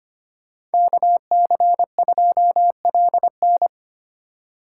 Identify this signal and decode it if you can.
Morse code, heard as a keyed tone, about 32 wpm, 700 Hz. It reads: KC2LN